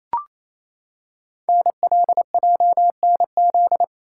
Morse code, heard as a keyed tone, about 29 wpm, 700 Hz. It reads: NLJNZ
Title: Code Groups - 5 Letters - 28wpm